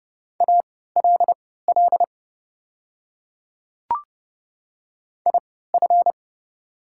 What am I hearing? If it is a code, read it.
Morse code, heard as a keyed tone, about 31 wpm, spaced as for 17 wpm, 690 Hz. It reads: ALL IF
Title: 3x Character Spacing - Top 200 Words - 30wpm